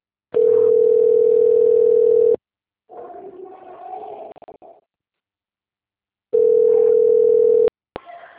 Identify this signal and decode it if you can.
Ringback tone (North America), 440 + 480 Hz.